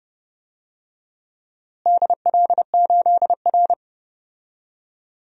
Morse code, heard as a keyed tone, about 30 wpm, 700 Hz. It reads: DL8R
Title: DXCC-100 Call Signs - 4 Characters - Repeated 2x - 30wpm